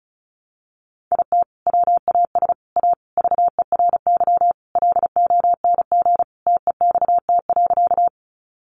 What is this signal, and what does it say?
Morse code, heard as a keyed tone, about 34 wpm, 700 Hz. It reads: IT WAS A VERY LONG TEXT.
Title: Sentences from Top 1200 Words 35wpm